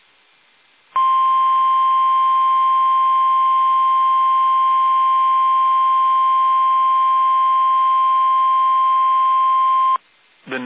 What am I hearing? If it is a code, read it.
Emergency Alert System (SAME) tones (the 1050 Hz attention tone), with speech. It is heard from 0.9 to 10.0 s.
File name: eas / KUSO / KUSO